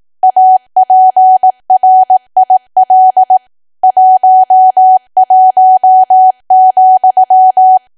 Morse code, heard as a keyed tone, about 18 wpm, 750 Hz. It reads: APRIL 11,
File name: cwa / W1AW-Level2 / 18WPM